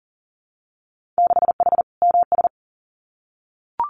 Morse code, heard as a keyed tone, about 40 wpm, 700 Hz. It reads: 6H MS
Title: Winter Field Day Exchanges Encore 40wpm